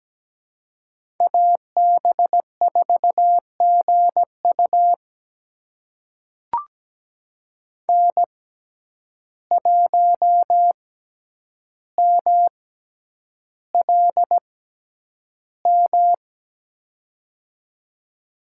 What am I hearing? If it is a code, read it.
Morse code, heard as a keyed tone, about 17 wpm, spaced as for 9 wpm, 700 Hz. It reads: AB4GU N1MLM
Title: USA Call Signs - 5 Characters - 6x Character Spacing - 17wpm